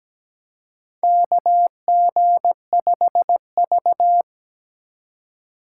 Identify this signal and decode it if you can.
Morse code, heard as a keyed tone, about 17 wpm, 700 Hz. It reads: KG5V
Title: K1USN SST QSO Practice - 17wpm